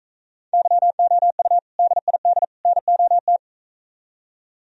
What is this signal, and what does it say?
Morse code, heard as a keyed tone, about 45 wpm, 700 Hz. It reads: YOU DID NOT